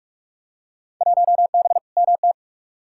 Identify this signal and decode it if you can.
Morse code, heard as a keyed tone, about 42 wpm, 700 Hz. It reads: 1B MT
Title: Field Day Exchanges - 45wpm